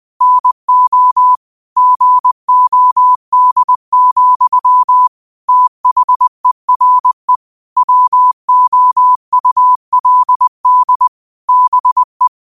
Morse code, heard as a keyed tone, about 20 wpm, 1000 Hz. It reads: NO GOD, THERE WOULD BE
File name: qotdfiles / 20wpm